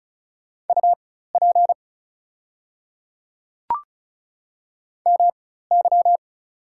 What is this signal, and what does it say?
Morse code, heard as a keyed tone, about 36 wpm, spaced as for 16 wpm, 700 Hz. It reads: UP MY
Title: 4x Character Spacing - Top 300 Words - 35wpm